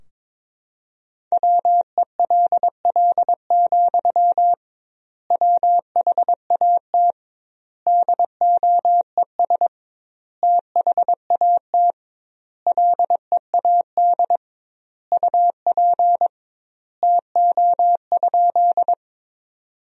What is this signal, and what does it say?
Morse code, heard as a keyed tone, about 22 wpm, 700 Hz. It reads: WELL, WHAT DOES THAT LEAD UP TO?
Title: 2x Word-Spacing - Sentences from Top 500 Words - 22wpm